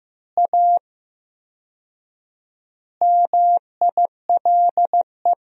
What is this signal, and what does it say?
Morse code, heard as a keyed tone, about 15 wpm, 700 Hz. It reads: A MILE
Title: 4x Word-Spacing - Sentences from Top 500 Words - 15wpm